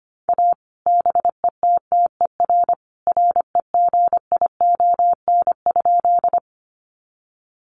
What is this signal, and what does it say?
Morse code, heard as a keyed tone, about 25 wpm, 700 Hz. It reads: A BETTER REGION?
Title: Sentences from Top 1500 Words 25wpm